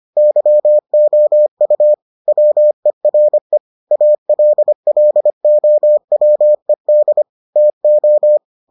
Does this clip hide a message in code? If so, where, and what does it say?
Yes: Morse code from 0.2 to 8.4 s, heard as a keyed tone, about 25 wpm, 600 Hz. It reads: YOU WERE ALLOWED TO